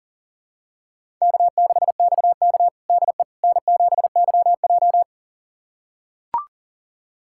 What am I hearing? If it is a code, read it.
Morse code, heard as a keyed tone, about 40 wpm, 700 Hz. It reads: K6XK DE N7YJ
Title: USA Callsign DE Callsign - 4 Characters - 40wpm